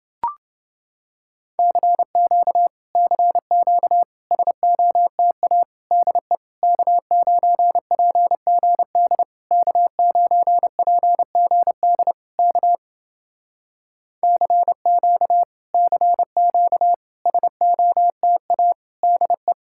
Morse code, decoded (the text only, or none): CQ CQ SOTA DE K9PGD K9PGD K CQ CQ SOTA DE